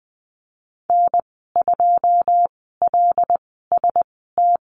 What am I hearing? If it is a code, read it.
Morse code, heard as a keyed tone, about 20 wpm, spaced as for 14 wpm, 700 Hz. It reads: N2LST
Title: 082 Lesson 14g - Number 2 - Call Signs 20wpm